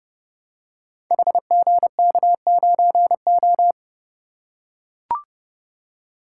Morse code, heard as a keyed tone, about 29 wpm, 700 Hz. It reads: HGK9O
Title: Code Groups - 5 Letter:Numbers - 30wpm